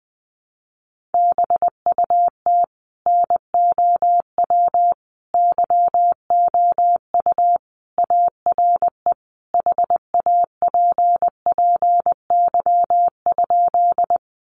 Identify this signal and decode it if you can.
Morse code, heard as a keyed tone, about 19 wpm, 700 Hz. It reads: BUT NOW YOU ARE HAPPY?
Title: Sentences from Top 1000 Words 20wpm